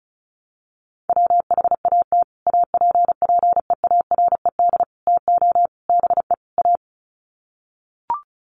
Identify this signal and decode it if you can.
Morse code, heard as a keyed tone, about 34 wpm, 700 Hz. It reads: WHAT APPEARED TO BE A
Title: Sets of 5 Words Encore2 35wpm